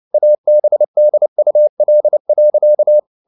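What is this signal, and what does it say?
Morse code, heard as a keyed tone, about 29 wpm, 600 Hz. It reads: ABDUL.
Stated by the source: Morse code at 29 wpm, sending Ken Follett's novel Never in English, from CW med Gnister.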